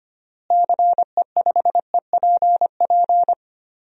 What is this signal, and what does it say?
Morse code, heard as a keyed tone, about 25 wpm, 700 Hz. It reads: CE5EPP
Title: DXCC-100 Call Signs - 6 Characters - Repeated 3x - 25wpm